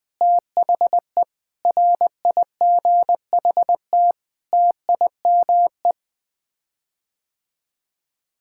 Morse code, heard as a keyed tone, about 20 wpm, 700 Hz. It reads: THE RIGHT TIME
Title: Sets of 8 Words - 20wpm